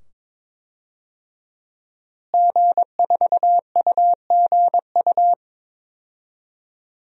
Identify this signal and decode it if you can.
Morse code, heard as a keyed tone, about 22 wpm, 700 Hz. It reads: G4UGU
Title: DXCC-100 Call Signs - 5 Characters - Repeated 2x - 22wpm